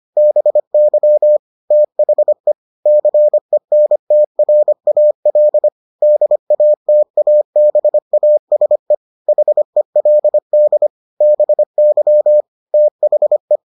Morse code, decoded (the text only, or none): BY THE CENTRAL DATABASE HELD BY THE